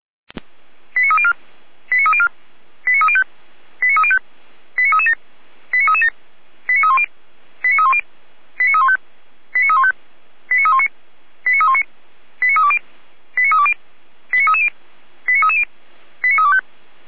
Five-tone selective call (ZVEI), heard as quick runs of five tones, tones of about 70 ms. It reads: 89304 89304 89305 89305 89337 89337 89310 89310 89315 89315 89319 89319 89320 89320 89330 89330 89325